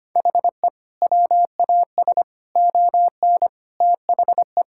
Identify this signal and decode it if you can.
Morse code, heard as a keyed tone, about 25 wpm, 700 Hz. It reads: HE WAS ON THE
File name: practice / 25wpm / Sentences with 4-Word Phrases - 25wpm